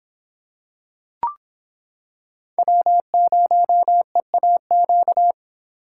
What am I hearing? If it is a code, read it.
Morse code, heard as a keyed tone, about 26 wpm, 700 Hz. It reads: W0EAQ